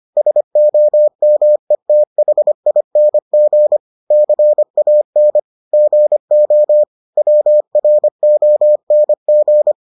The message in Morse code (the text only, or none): SOMETHING CAN GO WRONG